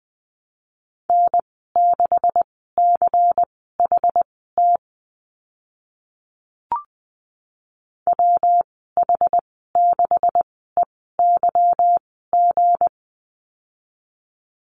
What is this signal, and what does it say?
Morse code, heard as a keyed tone, about 20 wpm, spaced as for 14 wpm, 700 Hz. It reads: N6CHT WH6EYG
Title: 167 Lesson 27g - Number 7 - Call Signs 20wpm